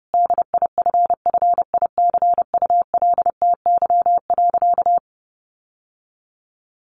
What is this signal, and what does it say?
Morse code, heard as a keyed tone, about 30 wpm, 700 Hz. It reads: DIFFICULTY.
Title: Sentences from Top 1500 Words 30wpm